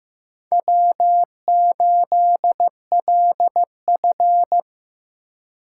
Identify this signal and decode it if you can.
Morse code, heard as a keyed tone, about 15 wpm, 700 Hz. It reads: W8LF
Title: USA Call Signs - 4 Characters - 4x Character Spacing - 15wpm